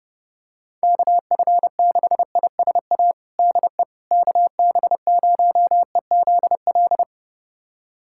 Morse code, heard as a keyed tone, about 30 wpm, 700 Hz. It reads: KF6ISA DE KB0EZL
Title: USA Callsign DE Callsign - 6 Characters - 30wpm